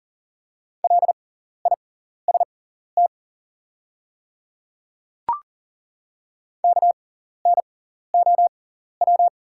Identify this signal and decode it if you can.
Morse code, heard as a keyed tone, about 40 wpm, spaced as for 14 wpm, 700 Hz. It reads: LIST KNOW
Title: 6x Character Spacing - Top 500 Words - 40wpm